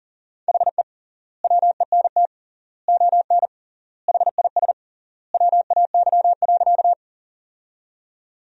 Morse code, heard as a keyed tone, about 40 wpm, 700 Hz. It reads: HE WENT ON HIS WAY.
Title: 3x Word-Spacing - Sentences from Top 200 Words - 40wpm